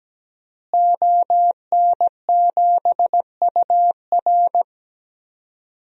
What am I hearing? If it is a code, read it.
Morse code, heard as a keyed tone, about 17 wpm, 700 Hz. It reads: ON7UR
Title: DXCC-100 Call Signs - 5 Characters - Repeated 3x - 17wpm